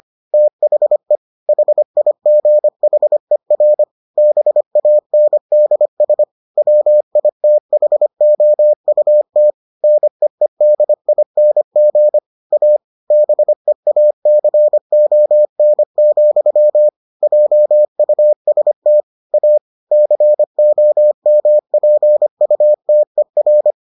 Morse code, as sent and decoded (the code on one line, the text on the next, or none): - .... / .... .. --. .... .-. / -... .- -. -.. ... / .-- .. - .... --- ..- - / -. . -.. .. -. --. / .- / -... .- -.-. --- -. --..-- / .--- ..- ... - / .- / -.-. --- -- .--. ..- - . .-.
THE HIGHER BANDS WITHOUT NEEDING A BEACON, JUST A COMPUTER